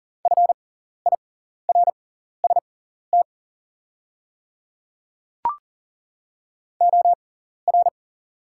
Morse code, heard as a keyed tone, about 39 wpm, spaced as for 14 wpm, 700 Hz. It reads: FIRST OR
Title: 6x Character Spacing - Top 100 Words - 40wpm